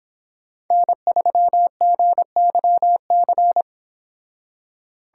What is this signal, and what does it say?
Morse code, heard as a keyed tone, about 26 wpm, 700 Hz. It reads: N3GYC